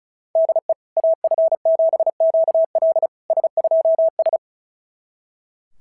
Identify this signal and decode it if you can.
Morse code, heard as a keyed tone, about 34 wpm, 650 Hz. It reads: DE AF7QL S2S
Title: SOTA QSO Practice 35wpm